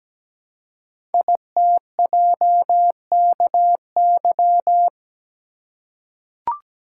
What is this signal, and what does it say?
Morse code, heard as a keyed tone, about 17 wpm, 700 Hz. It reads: ITJKY
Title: Code Groups - 5 Letters - 17wpm